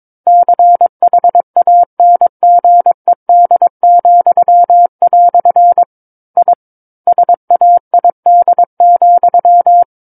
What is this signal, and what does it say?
Morse code, heard as a keyed tone, about 22 wpm, 700 Hz. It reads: CHANGED," I SAID,
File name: Book One / 13-Chapter 13 - 22 WPM